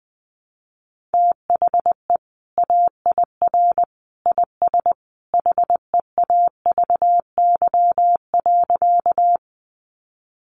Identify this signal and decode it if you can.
Morse code, heard as a keyed tone, about 20 wpm, 700 Hz. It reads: THE AIR IS HEAVY.
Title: Sentences from Top 1100 Words 20wpm